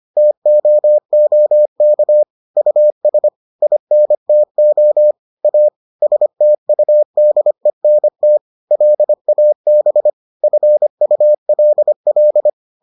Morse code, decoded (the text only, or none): TOOK US INTO A STUDENT LAB FULL